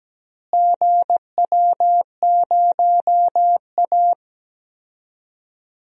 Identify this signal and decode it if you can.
Morse code, heard as a keyed tone, about 17 wpm, 700 Hz. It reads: GW0A
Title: DXCC-100 Call Signs - 4 Characters - Repeated 2x - 17wpm